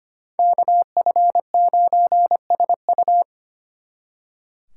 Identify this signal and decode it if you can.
Morse code, heard as a keyed tone, about 33 wpm, 700 Hz. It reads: KF9SU